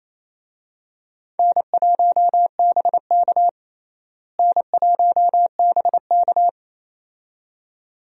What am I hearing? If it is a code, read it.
Morse code, heard as a keyed tone, about 28 wpm, 700 Hz. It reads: N1BK N1BK
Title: USA Call Signs - 4 Characters - Repeated 2x - 28wpm